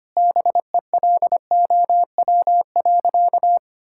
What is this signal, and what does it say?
Morse code, heard as a keyed tone, about 25 wpm, 700 Hz. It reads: BELOW.
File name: practice / 25wpm / Sentences with 9-Word Phrases - 25wpm